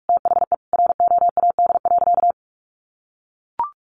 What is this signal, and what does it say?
Morse code, heard as a keyed tone, about 44 wpm, 700 Hz. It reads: THE ROAD.
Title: Sentences from Top 400 Words 45wpm